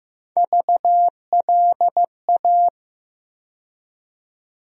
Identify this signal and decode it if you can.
Morse code, heard as a keyed tone, about 15 wpm, 700 Hz. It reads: VLA
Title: USA Call Sign Postfixes - 3 Letters 15wpm